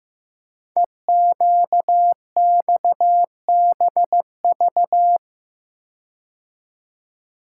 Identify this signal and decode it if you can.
Morse code, heard as a keyed tone, about 15 wpm, 700 Hz. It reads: EQXBV
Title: Code Groups - 5 Letter:Numbers - 15wpm